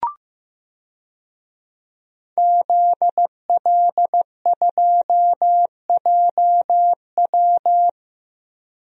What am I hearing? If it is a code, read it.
Morse code, heard as a keyed tone, about 15 wpm, 700 Hz. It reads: ZL2JW
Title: DXCC-100 Call Signs - 5 Characters - 15wpm